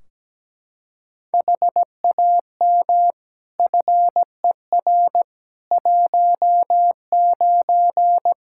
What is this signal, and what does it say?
Morse code, heard as a keyed tone, about 17 wpm, 700 Hz. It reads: HAM FER 19
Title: K1USN SST QSO Practice 17wpm